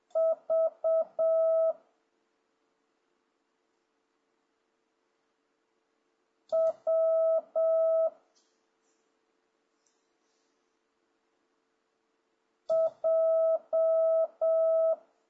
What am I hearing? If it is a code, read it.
Morse code, heard as a keyed tone, about 7 wpm, 650 Hz. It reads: V W J
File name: jh8cbh / 2026cw